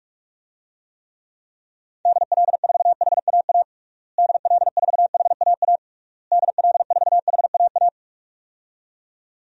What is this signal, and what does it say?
Morse code, heard as a keyed tone, about 46 wpm, 700 Hz. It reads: DL4HAA DL4HAA DL4HAA
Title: DXCC-100 Call Signs - 6 Characters - Repeated 3x - 45wpm